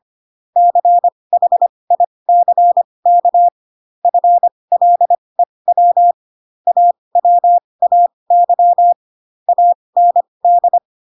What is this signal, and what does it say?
Morse code, heard as a keyed tone, about 25 wpm, spaced as for 20 wpm, 700 Hz. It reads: CHICK FLEW AWAY AND
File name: Pinocchio / Pinocchio ch7 15 25 ik1hge